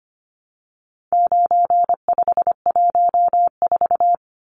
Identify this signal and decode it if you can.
Morse code, heard as a keyed tone, about 25 wpm, 700 Hz. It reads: 9514
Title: Numbers - 4-Digit 25wpm